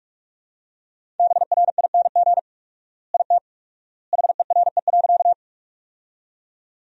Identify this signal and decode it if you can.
Morse code, heard as a keyed tone, about 47 wpm, 700 Hz. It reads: BRING IT HERE.
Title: 4x Word-Spacing - Sentences from Top 500 Words - 45wpm